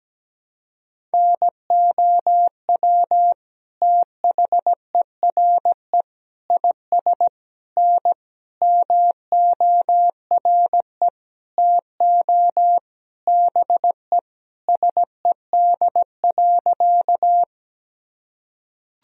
Morse code, heard as a keyed tone, about 17 wpm, 700 Hz. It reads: NOW THERE IS N MORE TO BE SED.
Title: Sentences from Top 100 Words and Abbreviations 17wpm